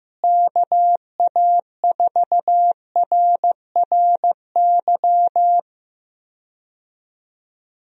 Morse code, heard as a keyed tone, about 15 wpm, 700 Hz. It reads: KA4RRY
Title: USA Call Signs - 6 Characters - Repeated 2x - 15wpm